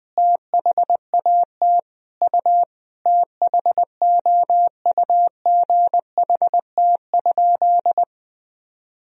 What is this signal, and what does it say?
Morse code, heard as a keyed tone, about 20 wpm, 700 Hz. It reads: THAT U THOUGHT?
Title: Sentences from Top 300 Words and Abbreviations 20wpm